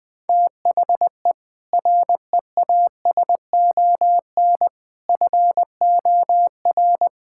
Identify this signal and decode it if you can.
Morse code, heard as a keyed tone, about 20 wpm, 700 Hz. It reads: THE REASON FOR
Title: Sentences with 7-Word Phrases - 20wpm